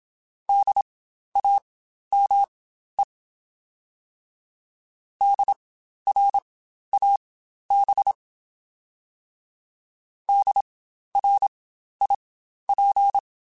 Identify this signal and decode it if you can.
Morse code, heard as a keyed tone, about 27 wpm, spaced as for 12 wpm, 800 Hz. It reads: DAME DRAB DRIP
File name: Spaced Alphas / Spaced Alphas mp3s / Ds sp @25